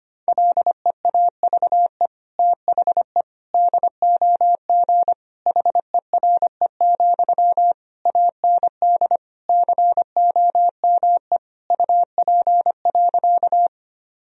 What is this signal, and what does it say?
Morse code, heard as a keyed tone, about 25 wpm, 700 Hz. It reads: LEAVE THE DOG HERE, AND COME UP.
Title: Sentences from Top 1000 Words - Review Part 2 - 25wpm